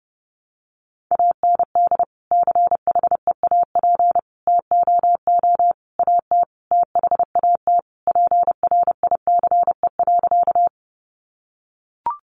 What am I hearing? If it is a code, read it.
Morse code, heard as a keyed tone, about 30 wpm, 700 Hz. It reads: AND CHEAP TOO AT THAT PRICE.
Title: Sentences from Top 1200 Words 30wpm